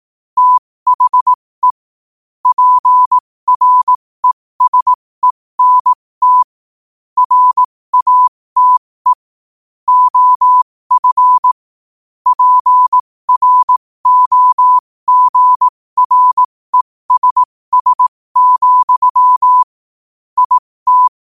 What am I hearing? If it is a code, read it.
Morse code, heard as a keyed tone, about 18 wpm, spaced as for 15 wpm, 1000 Hz. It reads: THE PRESENT RATE OF PROGRESS, IT
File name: qotdfiles / 15wpm